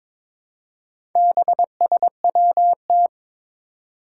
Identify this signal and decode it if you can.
Morse code, heard as a keyed tone, about 22 wpm, 700 Hz. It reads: BSWT
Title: Four Challenging Letter-Number Combinations 22wpm